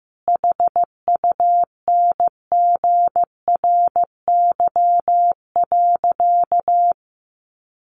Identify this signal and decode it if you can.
Morse code, heard as a keyed tone, about 15 wpm, 700 Hz. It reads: HUNGRY.
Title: Sentences from Top 1400 Words 15wpm